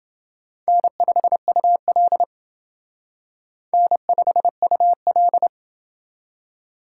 Morse code, heard as a keyed tone, about 28 wpm, 700 Hz. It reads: N5UL N5UL